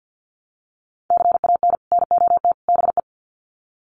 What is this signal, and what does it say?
Morse code, heard as a keyed tone, about 50 wpm, 700 Hz. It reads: CAN NOT BE
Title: Sets of 3 Words 50wpm